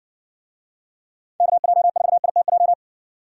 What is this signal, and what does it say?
Morse code, heard as a keyed tone, about 60 wpm, 700 Hz. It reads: KJ4ITJ